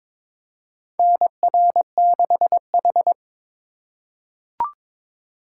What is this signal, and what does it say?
Morse code, heard as a keyed tone, about 22 wpm, 700 Hz. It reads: NR6H
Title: USA Call Signs - 4 Characters - 3x Character Spacing - 22wpm